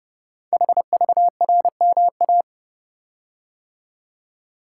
Morse code, heard as a keyed tone, about 30 wpm, 690 Hz. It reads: HVRMA
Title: Code Groups - 5 Letters - 30wpm